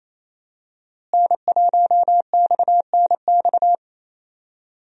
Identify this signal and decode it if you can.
Morse code, heard as a keyed tone, about 28 wpm, 700 Hz. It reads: N1XNX
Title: USA Call Signs - 5 Characters - Repeated 3x - 28wpm